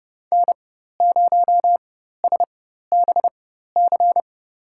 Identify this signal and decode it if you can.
Morse code, heard as a keyed tone, about 29 wpm, spaced as for 14 wpm, 700 Hz. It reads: N0SBC
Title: USA Call Signs - 5 Characters - 4x Character Spacing - 30wpm